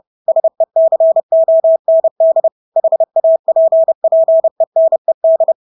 Morse code, heard as a keyed tone, about 30 wpm, 650 Hz. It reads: SECOND HAPPENED
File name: Engelske ord 30-30 wpm